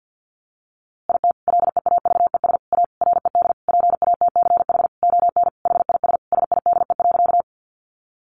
Morse code, heard as a keyed tone, about 50 wpm, 700 Hz. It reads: IT LEAVES A RED PATCH ON HIS SIDE.
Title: A Sound Of Thunder 50wpm